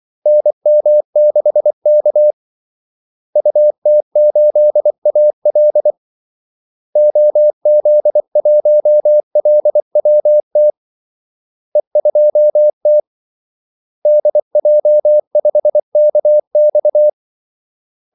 Morse code, heard as a keyed tone, about 24 wpm, 600 Hz. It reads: NM6K UT8AL OZ1LWT E2T DJ5KX
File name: Callsigns-24wpm